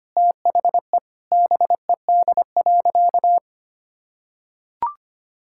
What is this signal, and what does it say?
Morse code, heard as a keyed tone, about 25 wpm, 700 Hz. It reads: THE BED.
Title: Sentences from Top 1000 Words - Review Part 6 - 25wpm